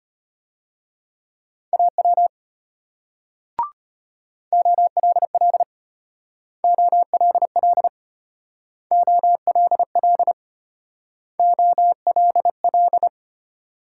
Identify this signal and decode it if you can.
Morse code, heard as a keyed tone, about 32 wpm, 700 Hz. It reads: AW OLL OLL OLL OLL